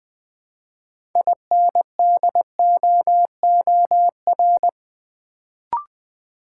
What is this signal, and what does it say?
Morse code, heard as a keyed tone, about 20 wpm, 700 Hz. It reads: INDOOR
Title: Build A Word - 2 Letters At A Time - 6-Letter Words - 20wpm